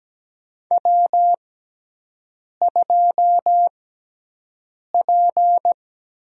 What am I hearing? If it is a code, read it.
Morse code, heard as a keyed tone, about 17 wpm, spaced as for 6 wpm, 700 Hz. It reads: W2P
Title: USA Call Signs - 5 Characters - 6x Character Spacing - 17wpm